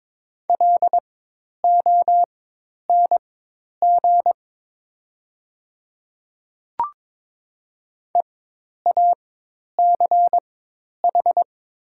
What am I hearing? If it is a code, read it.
Morse code, heard as a keyed tone, about 22 wpm, spaced as for 10 wpm, 700 Hz. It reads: LONG EACH
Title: 4x Character Spacing - Top 100 Words - 22wpm